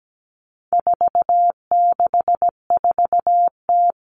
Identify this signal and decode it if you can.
Morse code, heard as a keyed tone, about 17 wpm, 700 Hz. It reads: 464T
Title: All Cut Numbers - 4-Digit 17wpm